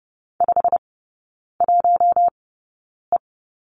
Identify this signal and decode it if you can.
Morse code, heard as a keyed tone, about 30 wpm, spaced as for 9 wpm, 700 Hz. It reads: 51E
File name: course / 30:10wpm / 087 Lesson 15e - Number 5 - Three Characters 30wpm